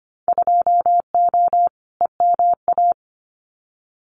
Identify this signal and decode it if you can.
Morse code, heard as a keyed tone, about 25 wpm, 700 Hz. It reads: 2O EMA
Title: Winter Field Day Exchanges Encore 25wpm